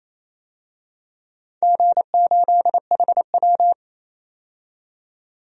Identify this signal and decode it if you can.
Morse code, heard as a keyed tone, about 28 wpm, 700 Hz. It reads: G8HW
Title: DXCC-100 Call Signs - 4 Characters - 28wpm